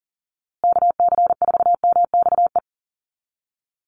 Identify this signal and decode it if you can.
Morse code, heard as a keyed tone, about 40 wpm, 700 Hz. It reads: KC4MXE
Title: USA Call Signs Repeated 3x 40wpm